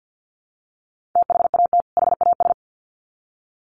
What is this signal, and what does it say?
Morse code, heard as a keyed tone, about 50 wpm, 700 Hz. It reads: THAT HAS
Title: Sets of 2 Words 50wpm